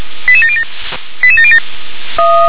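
Five-tone selective call (ZVEI), heard as quick runs of five tones, tones of about 70 ms. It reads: 88707 88707
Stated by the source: ZVEI five-tone selective call received by SDR.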